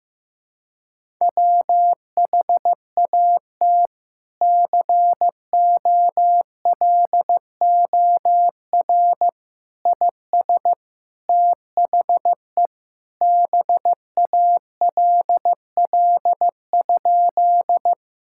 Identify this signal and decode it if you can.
Morse code, heard as a keyed tone, about 16 wpm, 700 Hz. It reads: WHAT COLOR IS THE BALL?
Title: Sentences from Top 1000 Words - Review Part 6 - 15wpm